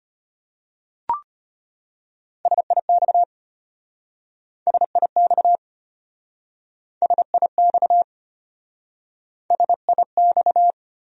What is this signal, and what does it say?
Morse code, heard as a keyed tone, about 34 wpm, 700 Hz. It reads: SIX SIX SIX SIX